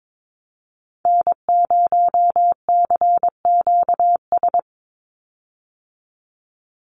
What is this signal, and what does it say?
Morse code, heard as a keyed tone, about 22 wpm, 700 Hz. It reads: N0CQS
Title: USA Call Signs 22wpm